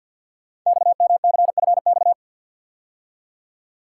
Morse code, heard as a keyed tone, about 50 wpm, 690 Hz. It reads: XMCFX
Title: Code Groups - 5 Letters - 50wpm